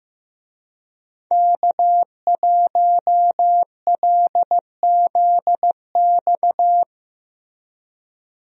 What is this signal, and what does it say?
Morse code, heard as a keyed tone, about 15 wpm, 700 Hz. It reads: K1LZX
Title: USA Call Signs - 5 Characters - Repeated 2x - 15wpm